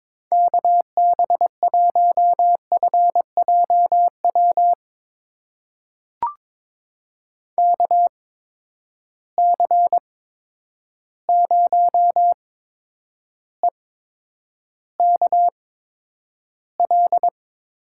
Morse code, heard as a keyed tone, about 22 wpm, spaced as for 9 wpm, 700 Hz. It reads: KB1FJW KC0EKL
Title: USA Call Signs - 6 Characters - 8x Character Spacing - 22wpm